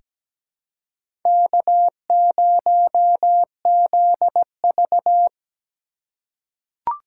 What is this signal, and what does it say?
Morse code, heard as a keyed tone, about 17 wpm, 700 Hz. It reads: K0ZV
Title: USA Call Signs - 4 Characters - 3x Character Spacing - 17wpm